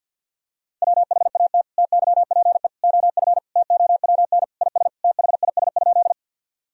Morse code, heard as a keyed tone, about 50 wpm, 700 Hz. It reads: WHAT TYPE OF TOWN IS THIS?
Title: Sentences from Top 1000 Words - Review Part 3 - 50wpm